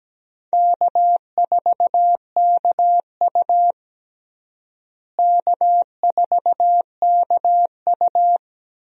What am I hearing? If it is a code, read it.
Morse code, heard as a keyed tone, about 17 wpm, 700 Hz. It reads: K4KU K4KU
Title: USA Call Signs - 4 Characters - Repeated 3x - 17wpm